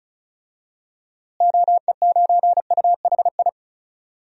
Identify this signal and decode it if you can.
Morse code, heard as a keyed tone, about 36 wpm, 700 Hz. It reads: OE9UHI
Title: DXCC-100 Call Signs - 6 Characters - Repeated 2x - 35wpm